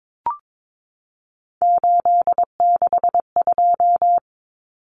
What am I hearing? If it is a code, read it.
Morse code, heard as a keyed tone, about 22 wpm, 700 Hz. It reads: 862
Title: Common Cut Numbers - 3-Digit 22wpm